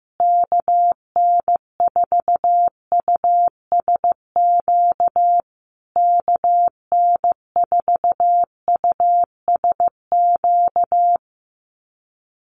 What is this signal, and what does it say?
Morse code, heard as a keyed tone, about 15 wpm, 700 Hz. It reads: KN4USQ KN4USQ
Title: USA Call Signs Repeated 3x 15wpm